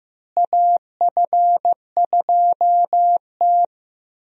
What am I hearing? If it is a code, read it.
Morse code, heard as a keyed tone, about 15 wpm, 700 Hz. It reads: AF2T
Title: USA Call Signs - 4 Characters - Repeated 3x - 15wpm